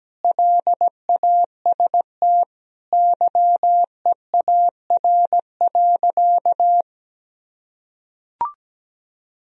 Morse code, heard as a keyed tone, about 17 wpm, 700 Hz. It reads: LAST YEAR.
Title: Sentences from Top 1000 Words - Review Part 1 - 17wpm